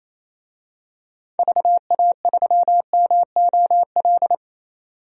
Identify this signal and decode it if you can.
Morse code, heard as a keyed tone, about 28 wpm, 700 Hz. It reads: VA3MOL
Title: DXCC-100 Call Signs - 6 Characters - 28wpm